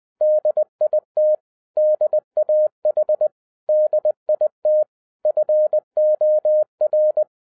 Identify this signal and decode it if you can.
Morse code, heard as a keyed tone, about 20 wpm, 610 Hz. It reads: DIT DAH DIT FOR